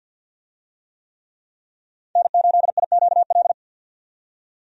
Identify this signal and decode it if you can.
Morse code, heard as a keyed tone, about 53 wpm, 700 Hz. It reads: N8IQL
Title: USA Call Signs - 5 Characters - 50wpm